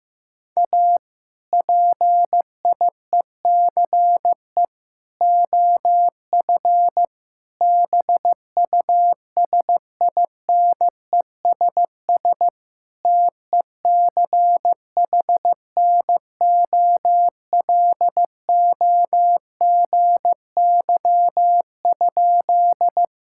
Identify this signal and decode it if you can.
Morse code, heard as a keyed tone, about 15 wpm, 700 Hz. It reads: A PIECE OF BUSINESS TECHNOLOGY?
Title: Sentences from Top 1000 Words - Review Part 5 - 15wpm